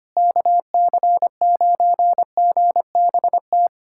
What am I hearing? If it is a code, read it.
Morse code, heard as a keyed tone, about 25 wpm, 700 Hz. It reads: KC9GBT
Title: USA Callsign DE Callsign - 6 Characters - 25wpm